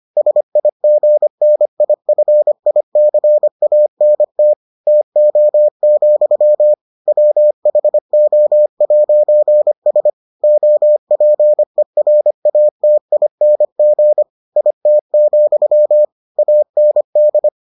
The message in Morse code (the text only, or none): SIGNIFICANT TO, WHO'S OPERATING IT, AND